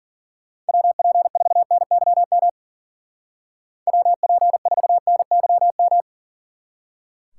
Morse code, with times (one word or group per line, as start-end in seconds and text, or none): 0.7-2.5 WP4NYM
3.9-6.0 WP4NYM